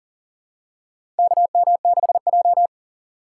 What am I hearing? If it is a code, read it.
Morse code, heard as a keyed tone, about 40 wpm, 700 Hz. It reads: KM6J
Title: USA Call Signs - 4 Characters - 2x Character Spacing - 40wpm